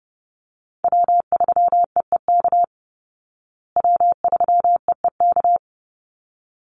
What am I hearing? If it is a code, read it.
Morse code, heard as a keyed tone, about 30 wpm, 700 Hz. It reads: W3EEK W3EEK
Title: Call Signs Repeated 3x - 30wpm